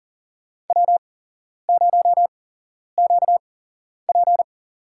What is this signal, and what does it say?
Morse code, heard as a keyed tone, about 40 wpm, spaced as for 11 wpm, 700 Hz. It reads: W0QP
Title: USA Call Signs - 4 Characters - 8x Character Spacing - 40wpm